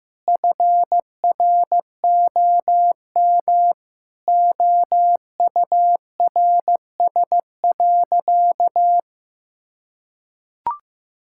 Morse code, heard as a keyed tone, about 16 wpm, 700 Hz. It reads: FROM OURS.
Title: Sentences from Top 1000 Words - Review Part 5 - 15wpm